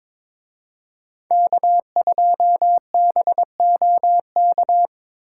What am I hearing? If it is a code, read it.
Morse code, heard as a keyed tone, about 26 wpm, 700 Hz. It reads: K2BOK
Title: Speed Racing - USA 1x3 Call Signs 22wpm